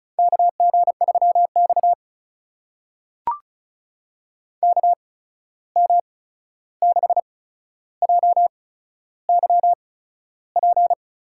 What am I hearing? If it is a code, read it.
Morse code, heard as a keyed tone, about 34 wpm, spaced as for 13 wpm, 700 Hz. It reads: KG3X KM6JYP
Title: USA Call Signs - 8x Character Spacing - 35wpm